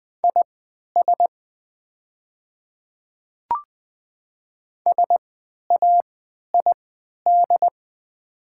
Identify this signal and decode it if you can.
Morse code, heard as a keyed tone, about 20 wpm, spaced as for 11 wpm, 700 Hz. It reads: IS SAID
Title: 3x Character Spacing - Top 100 Words - 20wpm